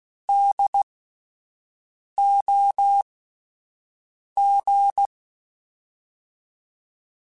Morse code, heard as a keyed tone, about 16 wpm, spaced as for 5 wpm, 800 Hz. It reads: DOG
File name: General / General-mp3s / Houses @10